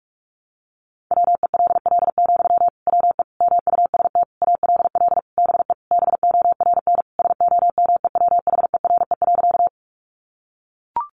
Morse code, heard as a keyed tone, about 44 wpm, 700 Hz. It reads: WELL, WE MUST ALL BE BORN SOMEWHERE.
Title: Sentences from Top 1400 Words 45wpm